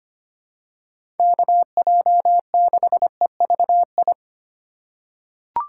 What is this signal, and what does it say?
Morse code, heard as a keyed tone, about 25 wpm, 700 Hz. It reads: KJ6EVI
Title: USA Call Signs - 6 Characters - 3x Character Spacing - 25wpm